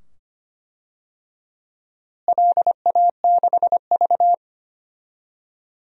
Morse code, heard as a keyed tone, about 25 wpm, 700 Hz. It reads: LA6V
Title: DXCC-100 Call Signs - 4 Characters - Repeated 2x - 25wpm